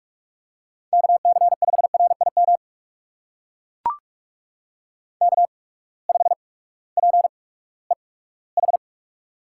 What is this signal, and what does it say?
Morse code, heard as a keyed tone, about 44 wpm, spaced as for 20 wpm, 700 Hz. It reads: KC5RIM K5PEH
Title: USA Call Signs - 8x Character Spacing - 45wpm